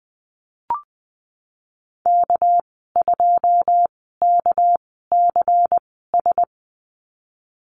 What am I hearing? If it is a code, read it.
Morse code, heard as a keyed tone, about 20 wpm, spaced as for 14 wpm, 700 Hz. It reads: K2KCS
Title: 231 Lesson 37g - Letter x - Call Signs 20wpm